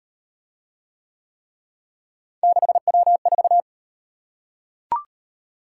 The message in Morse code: 6W4

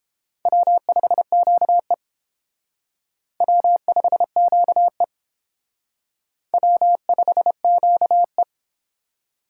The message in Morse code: W5QE W5QE W5QE

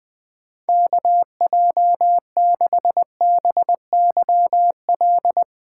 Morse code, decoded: KJ6BYL

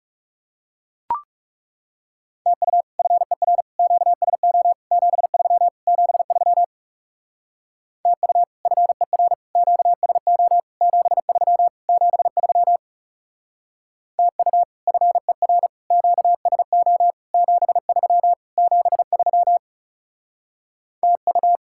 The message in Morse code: TU FER QSO 73 73 TU FER QSO 73 73 TU FER QSO 73 73 TU

